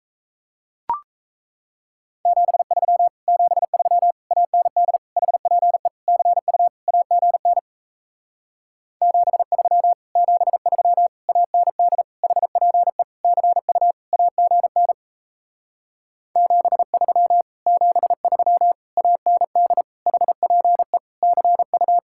Morse code, heard as a keyed tone, about 37 wpm, 700 Hz. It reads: 73 73 AND HPE CU AGN 73 73 AND HPE CU AGN 73 73 AND HPE CU